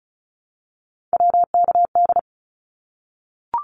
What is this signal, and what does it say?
Morse code, heard as a keyed tone, about 34 wpm, 700 Hz. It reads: WKD
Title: Abbreviations Repeated 3x - 35wpm